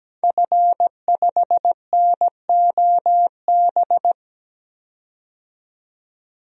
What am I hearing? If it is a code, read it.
Morse code, heard as a keyed tone, about 17 wpm, 700 Hz. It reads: F5NOB
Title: DXCC-100 Call Signs - 5 Characters - 17wpm